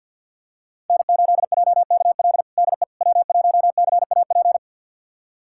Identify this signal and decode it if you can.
Morse code, heard as a keyed tone, about 51 wpm, 700 Hz. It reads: N8JKL DE W1CAP
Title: USA Callsign DE Callsign - 5 Characters - 50wpm